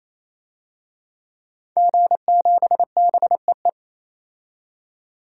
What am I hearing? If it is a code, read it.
Morse code, heard as a keyed tone, about 28 wpm, 700 Hz. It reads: G7BEE
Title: DXCC-100 Call Signs - 5 Characters - 28wpm